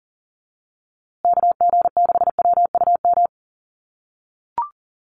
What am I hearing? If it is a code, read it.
Morse code, heard as a keyed tone, about 40 wpm, 700 Hz. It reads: KG6WUM
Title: USA Call Signs Repeated 3x 40wpm